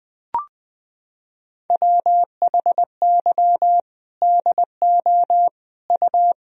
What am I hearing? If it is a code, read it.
Morse code, heard as a keyed tone, about 20 wpm, 700 Hz. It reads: WHY DO U
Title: Sentences from Top 300 Words and Abbreviations 20wpm